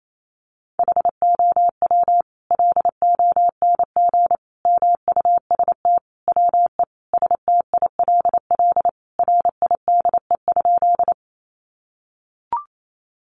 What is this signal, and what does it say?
Morse code, heard as a keyed tone, about 28 wpm, 700 Hz. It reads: HOW LONG MUST WE STILL RIDE?
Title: Sentences from Top 1200 Words 28wpm